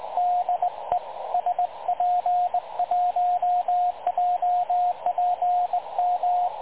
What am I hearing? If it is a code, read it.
Morse code, heard as a keyed tone, about 19 wpm, 720 Hz, with radio noise. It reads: DE SP1JPM
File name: dlazaawansowanych / nagraniazpasma / websdr recording 3565.0kHz